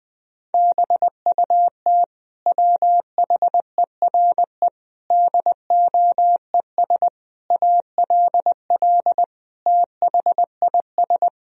Morse code, heard as a keyed tone, about 20 wpm, 700 Hz. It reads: BUT WHERE DOES ALL THIS
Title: Sentences from Top 1000 Words - Review Part 2 - 20wpm